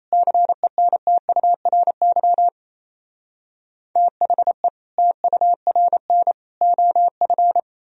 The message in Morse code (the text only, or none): CENTURY THE TURN OF